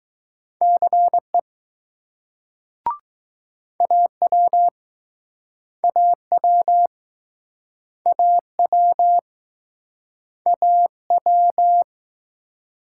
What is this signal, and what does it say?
Morse code, heard as a keyed tone, about 20 wpm, 700 Hz. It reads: CE AW AW AW AW